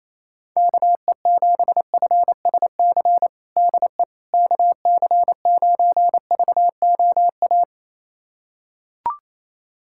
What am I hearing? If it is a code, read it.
Morse code, heard as a keyed tone, about 27 wpm, 700 Hz. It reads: KE7FSC DE KC9VOA
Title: USA Callsign DE Callsign - 6 Characters - 28wpm